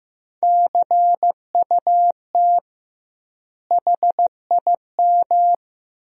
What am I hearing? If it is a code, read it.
Morse code, heard as a keyed tone, about 16 wpm, 700 Hz. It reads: CUT HIM
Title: 2x Word-Spacing - Sentences from Top 300 Words - 15wpm